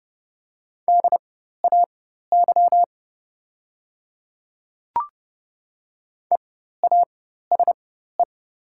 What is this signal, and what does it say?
Morse code, heard as a keyed tone, about 30 wpm, spaced as for 14 wpm, 700 Hz. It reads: DAY EASE
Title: 4x Character Spacing - Top 500 Words - 30wpm